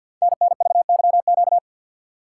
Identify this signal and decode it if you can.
Morse code, heard as a keyed tone, about 51 wpm, 700 Hz. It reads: NNVYQ
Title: Code Groups - 5 Letters - 50wpm